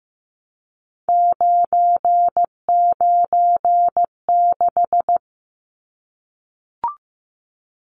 Morse code, heard as a keyed tone, about 15 wpm, 700 Hz. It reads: 996
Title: Numbers - 3-Digit 15wpm